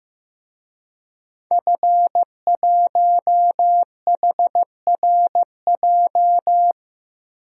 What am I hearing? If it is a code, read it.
Morse code, heard as a keyed tone, about 15 wpm, 700 Hz. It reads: F1HRJ
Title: DXCC-100 Call Signs - 5 Characters - 15wpm